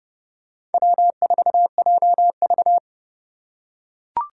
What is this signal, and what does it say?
Morse code, heard as a keyed tone, about 30 wpm, 700 Hz. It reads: W4JV
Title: USA Call Signs - 4 Characters - 2x Character Spacing - 30wpm